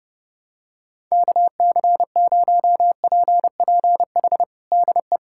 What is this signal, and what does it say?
Morse code, heard as a keyed tone, about 30 wpm, 700 Hz. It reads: KC0PPH DE
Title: USA Callsign DE Callsign 30wpm